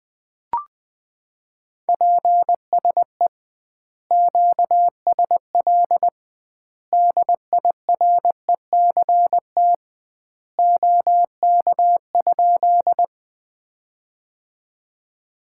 Morse code, heard as a keyed tone, about 21 wpm, 700 Hz. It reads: PSE QSL DIRECT OK?
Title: Common QSO Phrases - 2x Word-Spacing - 20wpm